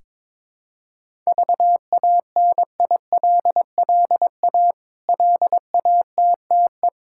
Morse code, heard as a keyed tone, about 21 wpm, 700 Hz. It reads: VANILLA LATTE
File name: Difficult Words with Hints - 22wpm